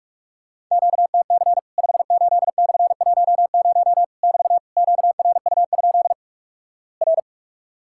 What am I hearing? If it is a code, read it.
Morse code, heard as a keyed tone, about 46 wpm, 700 Hz. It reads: QTC 58/10 <BT> QRU? R
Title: Worked All Europe DX Contest - QTC Exchange - 45wpm